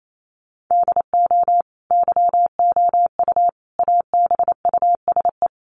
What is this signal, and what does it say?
Morse code, heard as a keyed tone, about 28 wpm, 700 Hz. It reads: DO YOU ABUSE
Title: Sentences from Top 1300 Words 28wpm